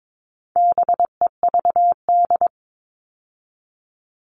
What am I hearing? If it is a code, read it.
Morse code, heard as a keyed tone, about 22 wpm, 700 Hz. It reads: BEVD
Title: All Cut Numbers - 4-Digit 22wpm